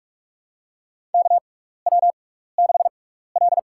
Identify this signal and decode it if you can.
Morse code, heard as a keyed tone, about 44 wpm, spaced as for 15 wpm, 700 Hz. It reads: KW6L
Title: USA Call Signs - 4 Characters - 6x Character Spacing - 45wpm